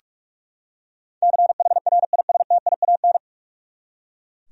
Morse code, heard as a keyed tone, about 44 wpm, 700 Hz. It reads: CHRISTIAN
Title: USA Common Names - 45wpm